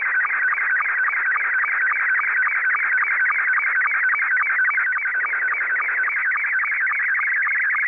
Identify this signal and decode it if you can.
SSTV (slow-scan television), Scottie 2.
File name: SSTV images / websdr recording start 14230.0kHz